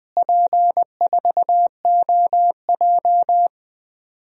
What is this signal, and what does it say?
Morse code, heard as a keyed tone, about 20 wpm, 700 Hz. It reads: P4OJ
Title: DXCC-100 Call Signs - 4 Characters - Repeated 3x - 20wpm